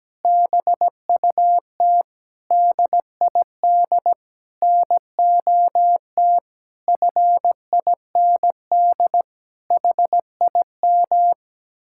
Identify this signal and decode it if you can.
Morse code, heard as a keyed tone, about 17 wpm, 700 Hz. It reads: BUT DID NOT FIND HIM
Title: Sentences from Top 1600 Words 17wpm